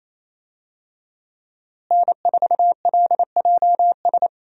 Morse code, heard as a keyed tone, about 28 wpm, 700 Hz. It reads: N4LJS
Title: USA Call Signs - 5 Characters - Repeated 3x - 28wpm